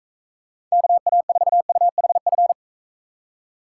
Morse code, heard as a keyed tone, about 42 wpm, 700 Hz. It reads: KA4UHF